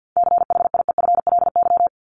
Morse code, heard as a keyed tone, about 50 wpm, 700 Hz. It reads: CHIEFLY
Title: LY Suffix - Top 100 Words with Base Words 50wpm